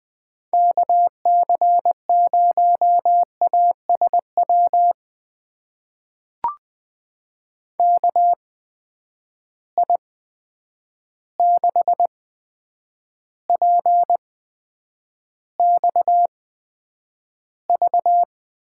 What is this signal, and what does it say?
Morse code, heard as a keyed tone, about 20 wpm, spaced as for 9 wpm, 700 Hz. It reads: KC0ASW KI6PXV
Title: USA Call Signs - 6 Characters - 8x Character Spacing - 20wpm